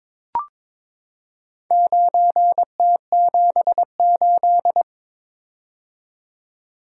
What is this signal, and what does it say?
Morse code, heard as a keyed tone, about 22 wpm, 700 Hz. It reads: 9T78
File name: practice / 22wpm / Four Letter-Number Combinations - 22wpm